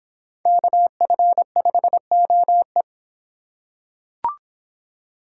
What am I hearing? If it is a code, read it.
Morse code, heard as a keyed tone, about 25 wpm, 700 Hz. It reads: KF5OE